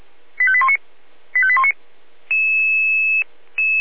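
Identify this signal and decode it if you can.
Five-tone selective call (ZVEI), heard as quick runs of five tones, tones of about 70 ms. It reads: 85819 85819
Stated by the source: ZVEI five-tone selective call received on an SDR.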